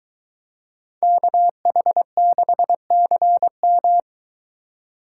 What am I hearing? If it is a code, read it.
Morse code, heard as a keyed tone, about 22 wpm, 700 Hz. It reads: KH6CM